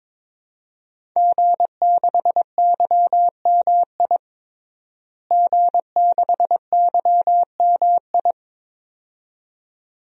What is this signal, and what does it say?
Morse code, heard as a keyed tone, about 22 wpm, 700 Hz. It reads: G6YMI G6YMI
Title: DXCC-100 Call Signs - 5 Characters - Repeated 2x - 22wpm